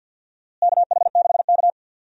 Morse code, heard as a keyed tone, about 50 wpm, 700 Hz. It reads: KH6K
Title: DXCC-100 Call Signs - 4 Characters - Repeated 2x - 50wpm